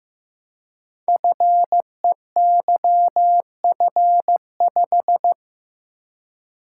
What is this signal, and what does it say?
Morse code, heard as a keyed tone, about 15 wpm, 700 Hz. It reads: FEYF5
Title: Code Groups - 5 Letter:Numbers - 15wpm